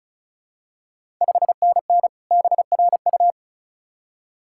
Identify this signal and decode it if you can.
Morse code, heard as a keyed tone, about 36 wpm, 690 Hz. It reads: 5NN BRU
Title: Ontario QSO Party Exchange 35wpm